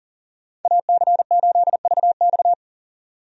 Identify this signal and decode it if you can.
Morse code, heard as a keyed tone, about 40 wpm, 700 Hz. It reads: AC8VX
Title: USA Call Signs - 5 Characters - 40wpm